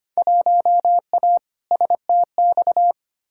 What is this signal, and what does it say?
Morse code, heard as a keyed tone, about 25 wpm, 700 Hz. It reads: 1A STX
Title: Field Day Exchanges - 25wpm